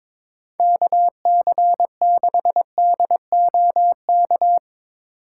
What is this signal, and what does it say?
Morse code, heard as a keyed tone, about 22 wpm, 700 Hz. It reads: KC6DOK
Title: USA Call Signs - 6 Characters - 8x Character Spacing - 22wpm